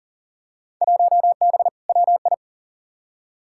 Morse code, heard as a keyed tone, about 38 wpm, 700 Hz. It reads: 1B WI
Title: Field Day Exchanges - 40wpm